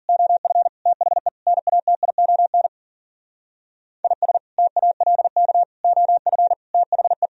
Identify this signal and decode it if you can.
Morse code, heard as a keyed tone, about 52 wpm, 700 Hz. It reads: OF THE NATION IS TALK OF THE